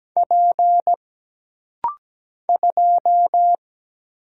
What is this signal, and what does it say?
Morse code, heard as a keyed tone, about 17 wpm, 700 Hz. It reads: P 2
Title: Single Letter-Number - Rapid-Fire 17wpm